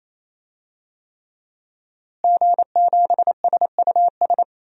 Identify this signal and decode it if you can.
Morse code, heard as a keyed tone, about 28 wpm, 700 Hz. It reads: G7SUS
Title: DXCC-100 Call Signs - 5 Characters - Repeated 3x - 28wpm